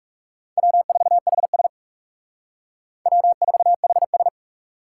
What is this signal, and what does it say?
Morse code, heard as a keyed tone, about 38 wpm, 700 Hz. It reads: W4HS W4HS
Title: Speed Racing - USA Call Signs 30wpm